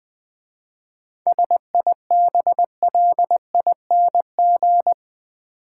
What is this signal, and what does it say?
Morse code, heard as a keyed tone, about 20 wpm, 700 Hz. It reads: SIBLING
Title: 7 Letter Words - First 3 Letters Given - 20wpm